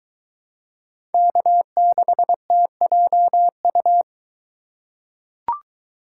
Morse code, heard as a keyed tone, about 22 wpm, 700 Hz. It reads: K6TJU